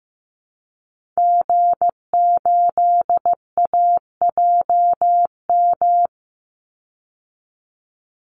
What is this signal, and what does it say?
Morse code, heard as a keyed tone, about 15 wpm, 700 Hz. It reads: G8AJM
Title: Call Signs Repeated 2x - 15wpm